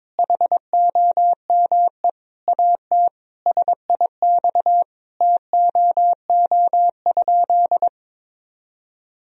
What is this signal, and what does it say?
Morse code, heard as a keyed tone, about 22 wpm, 700 Hz. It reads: HOME AT SIX TOO?
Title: Sentences from Top 1000 Words - Review Part 2 - 22wpm